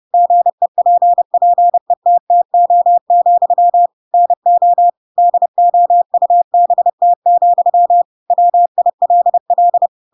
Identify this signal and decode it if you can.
Morse code, heard as a keyed tone, about 30 wpm, 700 Hz. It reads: GEPPETTO, NO DOUBT, WILL
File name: Pinocchio / Pinocchio ch3 30 30 ik1hge